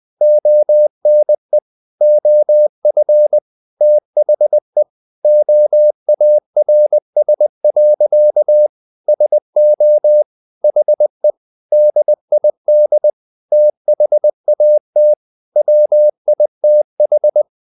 Morse code, heard as a keyed tone, about 20 wpm, 600 Hz. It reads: ONE OF THE OARS. SO HE DID THAT WITH
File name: the old man and the sea part 2 20wpm